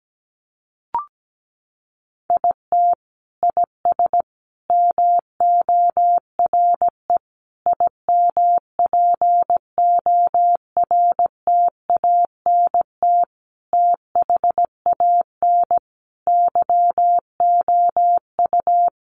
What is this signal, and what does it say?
Morse code, heard as a keyed tone, about 17 wpm, 700 Hz. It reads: IT IS MORE IMPORTANT THAN YOU
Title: Sentences from Top 600 Words 17wpm